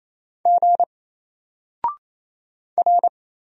Morse code, heard as a keyed tone, about 27 wpm, 700 Hz. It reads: G R
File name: Single-Letter-Number-28wpm